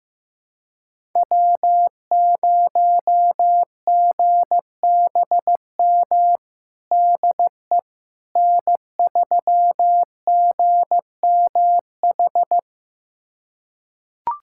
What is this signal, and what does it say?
Morse code, heard as a keyed tone, about 15 wpm, 700 Hz. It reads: W0GBM DE N3GMH
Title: USA Callsign DE Callsign - 5 Characters - 15wpm